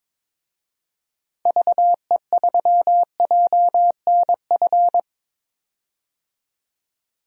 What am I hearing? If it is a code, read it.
Morse code, heard as a keyed tone, about 22 wpm, 700 Hz. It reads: VE3JNF
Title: DXCC-100 Call Signs - 6 Characters - 22wpm